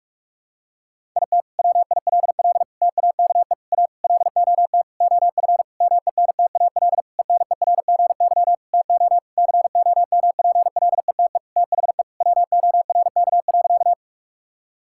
Morse code, heard as a keyed tone, about 46 wpm, 700 Hz. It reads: IT WILL TAKE A LOT OF MENTAL ENERGY TO COMPLETE THE WORK.
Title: Sentences from Top 1000 Words - Review Part 6 - 45wpm